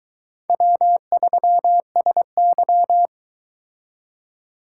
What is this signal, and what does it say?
Morse code, heard as a keyed tone, about 22 wpm, 700 Hz. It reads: W3SY